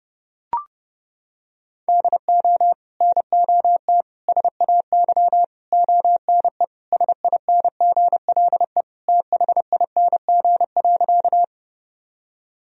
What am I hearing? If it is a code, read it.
Morse code, heard as a keyed tone, about 30 wpm, 700 Hz. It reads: DO NOT SAY ONE SINGLE THING.
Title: Sentences from Top 1000 Words - Review Part 4 - 30wpm